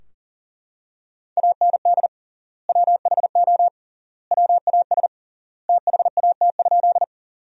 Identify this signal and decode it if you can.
Morse code, heard as a keyed tone, about 40 wpm, 700 Hz. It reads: AND WHO WAS THAT?
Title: 3x Word-Spacing - Sentences from Top 100 Words - 40wpm